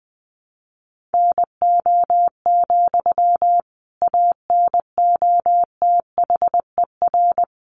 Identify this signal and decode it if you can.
Morse code, heard as a keyed tone, about 20 wpm, 700 Hz. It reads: NO, ANOTHER
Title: Sentences from Top 600 Words 20wpm